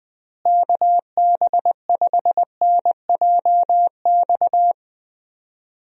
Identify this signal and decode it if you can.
Morse code, heard as a keyed tone, about 21 wpm, 700 Hz. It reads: KB5NJX
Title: USA Call Signs - 6 Characters - Repeated 2x - 20wpm